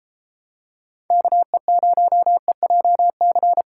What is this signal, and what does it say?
Morse code, heard as a keyed tone, about 31 wpm, 700 Hz. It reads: KE0EJC